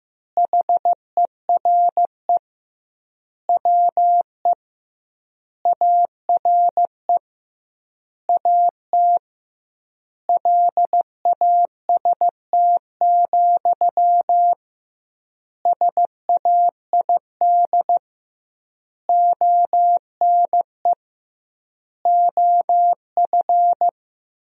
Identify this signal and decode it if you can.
Morse code, heard as a keyed tone, about 16 wpm, 700 Hz. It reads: HERE WE ARE AT LAST, SAID ONE OF
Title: 2x Word-Spacing - Sentences from Top 300 Words - 15wpm